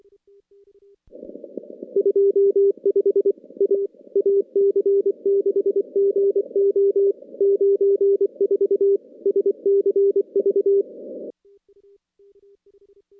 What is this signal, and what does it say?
Morse code, heard as a keyed tone, about 24 wpm, 400 Hz. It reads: 25 A AC6GO 94 SCV
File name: contests / CW-SS-k5tr- -2300